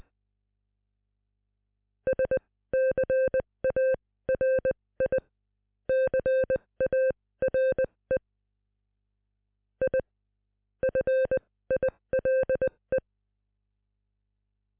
Morse code, heard as a keyed tone, about 20 wpm, spaced as for 15 wpm, 520 Hz. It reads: SCARICARE I FILE